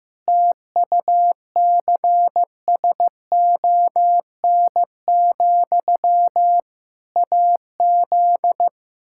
Morse code, heard as a keyed tone, about 15 wpm, 700 Hz. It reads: TUCSON, AZ
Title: Practice QSO - Exchange 1 - Beginning - 15wpm